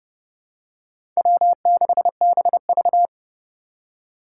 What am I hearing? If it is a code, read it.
Morse code, heard as a keyed tone, about 30 wpm, 700 Hz. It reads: W6BV
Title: Four Challenging Letter-Number Combinations 30wpm